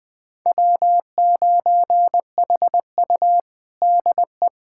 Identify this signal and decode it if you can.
Morse code, heard as a keyed tone, about 21 wpm, 700 Hz. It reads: W9HU DE
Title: USA Callsign DE Callsign - 4 Characters - 20wpm